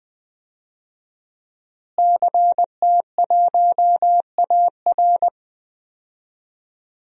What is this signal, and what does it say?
Morse code, heard as a keyed tone, about 20 wpm, 700 Hz. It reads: CT1AR
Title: DXCC-100 Call Signs - 5 Characters - Repeated 3x - 20wpm